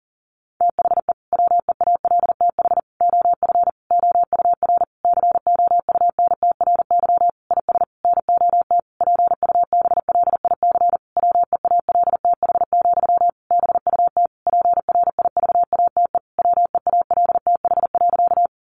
Morse code, heard as a keyed tone, about 40 wpm, 700 Hz. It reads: THE WEALTH OF OUR COUNTRY IS NOT PUBLIC WEALTH, BUT PRIVATE WEALTH.
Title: Sentences from Top 1200 Words 40wpm